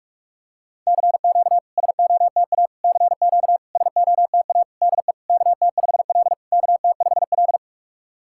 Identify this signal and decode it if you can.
Morse code, heard as a keyed tone, about 46 wpm, 700 Hz. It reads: CQ SOTA CQ SOTA DE KT5L KT5L
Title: SOTA QSO Practice 45wpm